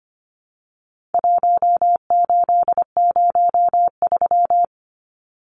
Morse code, heard as a keyed tone, about 25 wpm, 700 Hz. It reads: 1803
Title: Numbers - 4-Digit 25wpm